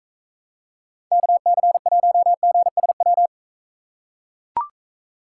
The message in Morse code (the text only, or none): KC1GSW